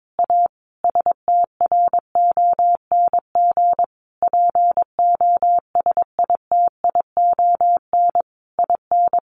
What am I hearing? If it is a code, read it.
Morse code, heard as a keyed tone, about 22 wpm, 700 Hz. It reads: A STRONG POSITION IN
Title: Sentences from Top 1200 Words 22wpm